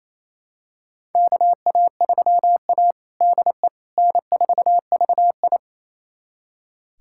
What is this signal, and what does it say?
Morse code, heard as a keyed tone, about 28 wpm, 700 Hz. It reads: KA3A DE N4VI
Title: USA Callsign DE Callsign - 4 Characters - 28wpm